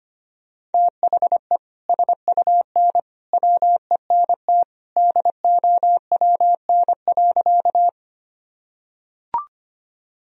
Morse code, heard as a keyed tone, about 25 wpm, 700 Hz. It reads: THE SUN WENT DOWN.
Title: Sentences from Top 1000 Words - Review Part 1 - 25wpm